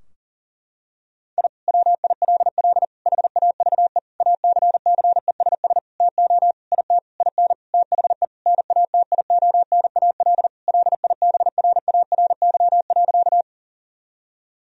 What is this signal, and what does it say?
Morse code, heard as a keyed tone, about 40 wpm, 700 Hz. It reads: I WILL HAVE ACCESS TO IT IN THE NATIONAL LIBRARY.
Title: Sentences from Top 1000 Words - Review Part 4 - 40wpm